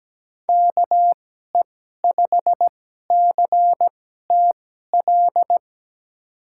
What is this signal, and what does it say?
Morse code, heard as a keyed tone, about 17 wpm, spaced as for 12 wpm, 700 Hz. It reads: KE5CTL
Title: USA Call Signs - 6 Characters - 2x Character Spacing - 17wpm